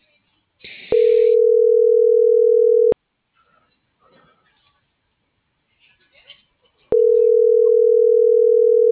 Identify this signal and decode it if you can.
Ringback tone (North America), 440 + 480 Hz.
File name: CENTRAL SAGOT / out danae.bsagot - .14553